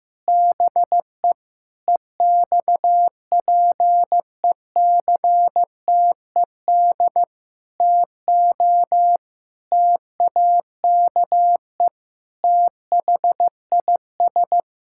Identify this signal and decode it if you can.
Morse code, heard as a keyed tone, about 15 wpm, 700 Hz. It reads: BE EXPECTED TO TAKE THIS
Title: Pangram Challenge v2 - 15wpm